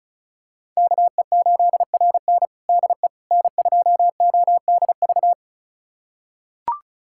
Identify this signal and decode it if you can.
Morse code, heard as a keyed tone, about 36 wpm, 700 Hz. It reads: KE8RN DE N2ODV
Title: USA Callsign DE Callsign - 5 Characters - 35wpm